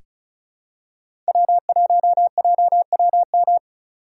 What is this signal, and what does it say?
Morse code, heard as a keyed tone, about 36 wpm, 700 Hz. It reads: W1JWM
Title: USA Call Signs - 5 Characters - 2x Character Spacing - 35wpm